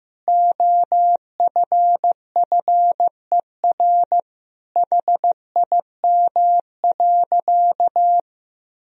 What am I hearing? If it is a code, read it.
Morse code, heard as a keyed tone, about 15 wpm, 700 Hz. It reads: OFFER HIM.
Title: Sentences from Top 1000 Words - Review Part 4 - 15wpm